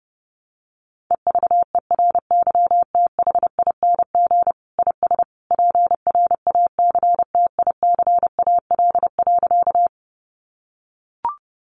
Morse code, heard as a keyed tone, about 30 wpm, 700 Hz. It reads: EVERYTHING IS PRACTICAL.
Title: Sentences from Top 1400 Words 30wpm